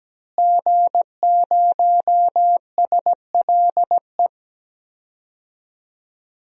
Morse code, heard as a keyed tone, about 17 wpm, 700 Hz. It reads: G0SLE
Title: DXCC-100 Call Signs - 5 Characters - Repeated 2x - 17wpm